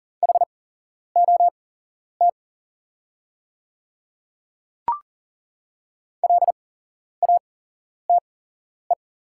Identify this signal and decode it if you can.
Morse code, heard as a keyed tone, about 39 wpm, spaced as for 11 wpm, 690 Hz. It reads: HOT LATE
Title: 8x Character Spacing - Top 500 Words - 40wpm